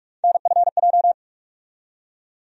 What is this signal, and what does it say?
Morse code, heard as a keyed tone, about 46 wpm, 700 Hz. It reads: NFJ
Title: USA Call Sign Postfixes - 3 Letters 45wpm